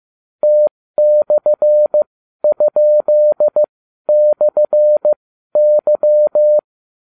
Morse code, heard as a keyed tone, about 15 wpm, spaced as for 12 wpm, 600 Hz. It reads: T/?/Y